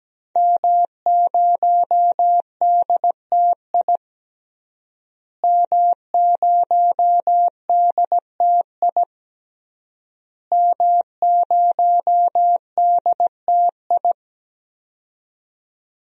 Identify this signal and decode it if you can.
Morse code, heard as a keyed tone, about 17 wpm, 700 Hz. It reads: M0DTI M0DTI M0DTI
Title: DXCC-100 Call Signs - 5 Characters - Repeated 3x - 17wpm